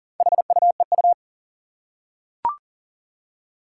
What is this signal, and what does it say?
Morse code, heard as a keyed tone, about 40 wpm, 690 Hz. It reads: HUEU
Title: Four Challenging Letter-Number Combinations 40wpm